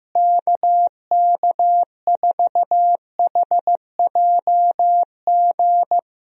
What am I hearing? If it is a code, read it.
Morse code, heard as a keyed tone, about 15 wpm, 700 Hz. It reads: KK4HJG
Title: USA Call Signs - 6 Characters - Repeated 3x - 15wpm